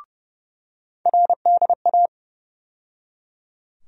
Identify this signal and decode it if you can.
Morse code, heard as a keyed tone, about 30 wpm, 700 Hz. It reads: RDA
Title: USA Call Sign Postfixes - 3 Letters 30wpm